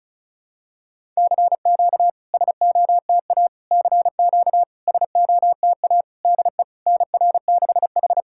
Morse code, heard as a keyed tone, about 36 wpm, 690 Hz. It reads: CQ SOTA CQ SOTA DE NR6H
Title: SOTA QSO Practice 35wpm